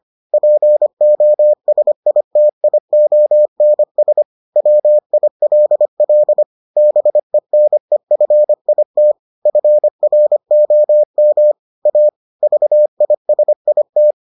Morse code, heard as a keyed tone, about 25 wpm, 600 Hz. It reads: POSITIONS WILL BENEFIT FROM A VISIT